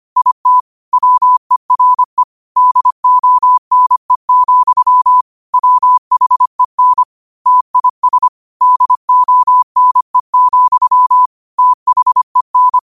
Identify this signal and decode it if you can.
Morse code, heard as a keyed tone, about 25 wpm, 1000 Hz. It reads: IT WERE DONE, WHEN TIS DONE, THEN
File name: qotdfiles / 25wpm